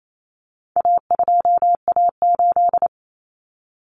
Morse code, heard as a keyed tone, about 28 wpm, 700 Hz. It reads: A2A8
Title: Common Cut Numbers - 4-Digit 28wpm